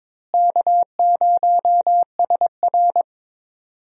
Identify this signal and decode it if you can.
Morse code, heard as a keyed tone, about 22 wpm, 700 Hz. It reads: K0SR
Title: USA Call Signs - 4 Characters - Repeated 2x - 22wpm